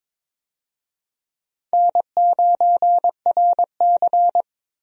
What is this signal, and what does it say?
Morse code, heard as a keyed tone, about 22 wpm, 700 Hz. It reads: N9RC
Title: USA Call Signs - 4 Characters - Repeated 3x - 22wpm